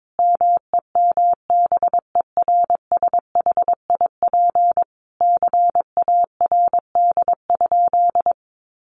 Morse code, heard as a keyed tone, about 22 wpm, 700 Hz. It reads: MEMBERSHIP CARD?
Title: Sentences from Top 1400 Words 22wpm